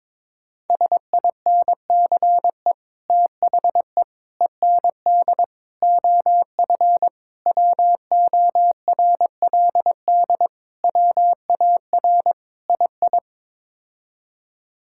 Morse code, heard as a keyed tone, about 22 wpm, 700 Hz. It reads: SINCE THE END OF WORLD WAR II
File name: practice / 22wpm / Sets of 7 Words - 22wpm